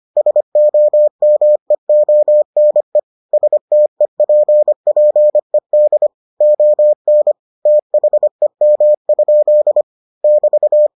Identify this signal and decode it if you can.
Morse code, heard as a keyed tone, about 25 wpm, 600 Hz. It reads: SOMEONE STEPPED ON THEM? <BT>